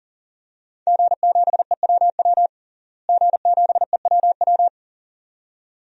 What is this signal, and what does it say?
Morse code, heard as a keyed tone, about 40 wpm, 700 Hz. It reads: G7EWW G7EWW
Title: DXCC-100 Call Signs - 5 Characters - Repeated 2x - 40wpm